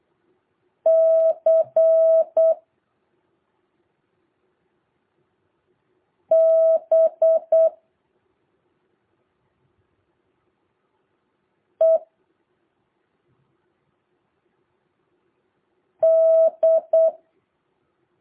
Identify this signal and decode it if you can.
Morse code, heard as a keyed tone, about 8 wpm, 650 Hz. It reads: C B E D
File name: jh8cbh / 2023cw